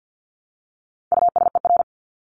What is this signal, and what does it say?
Morse code, heard as a keyed tone, about 50 wpm, 700 Hz. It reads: USER
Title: Top 751-1000 Words 50wpm